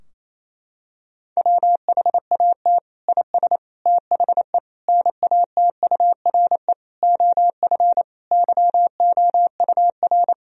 Morse code, heard as a keyed tone, about 28 wpm, 700 Hz. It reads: WHAT IS THE NATURE OF YOUR
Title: Sentences from Top 1000 Words - Review Part 4 - 28wpm